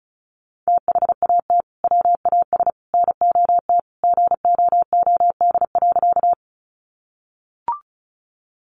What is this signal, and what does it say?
Morse code, heard as a keyed tone, about 34 wpm, 700 Hz. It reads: THAT WAS NOT GOOD.
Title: Sentences from Top 600 Words 35wpm